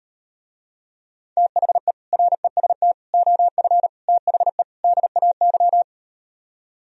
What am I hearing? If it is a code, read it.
Morse code, heard as a keyed tone, about 39 wpm, 700 Hz. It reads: THE REST OF THE DAY